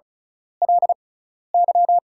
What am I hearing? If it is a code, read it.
Morse code, heard as a keyed tone, about 34 wpm, spaced as for 12 wpm, 700 Hz. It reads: LY
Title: 6x Character Spacing - Top 200 Words - 35wpm